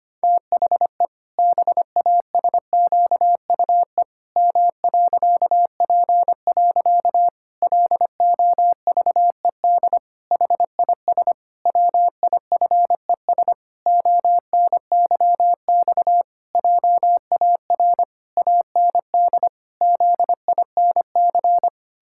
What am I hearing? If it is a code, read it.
Morse code, heard as a keyed tone, about 25 wpm, 700 Hz. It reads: THE BASQUE M.P. LOVED HIS WIFES ONYX JAR AND ZINC
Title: Pangram Challenge v2 - 25wpm